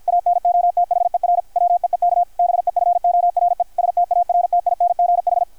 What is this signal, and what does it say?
Morse code, heard as a keyed tone, about 53 wpm, 720 Hz, with radio noise. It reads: ANOTHER WEEK BEFORE STARTINGH
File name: archived / CRSnet